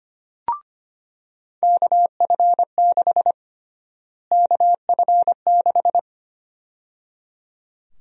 Morse code, heard as a keyed tone, about 25 wpm, 700 Hz. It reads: KF6 KF6
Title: USA Call Sign Prefixes - Repeated 2x - 25wpm